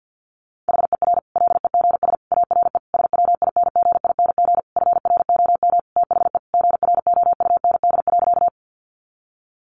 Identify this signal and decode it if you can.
Morse code, heard as a keyed tone, about 50 wpm, 700 Hz. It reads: HER LEGS ARE SWINGING FROM THE GROUND.
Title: Easy Phrases 50wpm